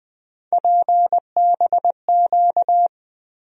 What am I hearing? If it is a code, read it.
Morse code, heard as a keyed tone, about 20 wpm, 700 Hz. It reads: PBQ
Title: USA Call Sign Postfixes - 3 Letters 20wpm